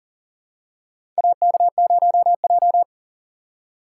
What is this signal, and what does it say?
Morse code, heard as a keyed tone, about 40 wpm, 700 Hz. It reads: AK0J
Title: USA Call Signs - 4 Characters - 2x Character Spacing - 40wpm